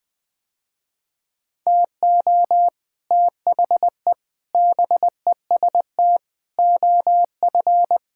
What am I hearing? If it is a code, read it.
Morse code, heard as a keyed tone, about 20 wpm, 700 Hz. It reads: TO THE BEST OF
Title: Sentences with 6-Word Phrases - Spoken Beginning - 20wpm